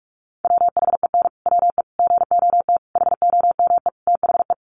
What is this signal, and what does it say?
Morse code, heard as a keyed tone, about 44 wpm, 700 Hz. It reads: WHEN WE GOT HOME THE
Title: Sentences from Top 500 Words 45wpm